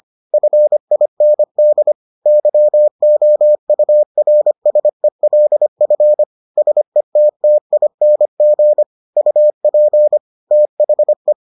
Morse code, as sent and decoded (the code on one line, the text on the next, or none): ..-. .. -. -.. / -.-- --- ..- .-. ... .-.. ..-. / ... - - .. -. --. / ..- .--. / - .... .
FIND YOURSELF SETTING UP THE